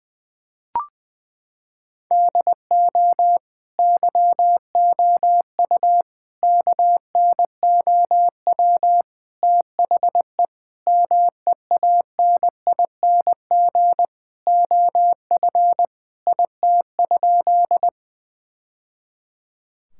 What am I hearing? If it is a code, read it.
Morse code, heard as a keyed tone, about 20 wpm, 700 Hz. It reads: DO YOU KNOW THE MEANING OF IT?
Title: Sentences from Top 1000 Words - Review Part 6 - 20wpm